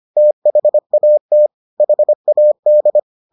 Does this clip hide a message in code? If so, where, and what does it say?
Yes: Morse code from 0.2 to 3.0 s, heard as a keyed tone, about 25 wpm, 600 Hz. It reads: THAT HAD